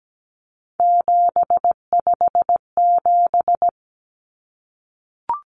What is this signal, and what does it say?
Morse code, heard as a keyed tone, about 17 wpm, 700 Hz. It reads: 757
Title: Common Cut Numbers - 3-Digit 17wpm